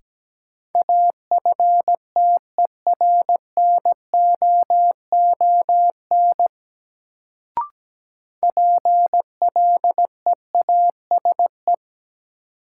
Morse code, heard as a keyed tone, about 17 wpm, 700 Hz. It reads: AFTERNOON PLEASE
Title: Sentences from Top 1500 Words - New Word Drill - 17wpm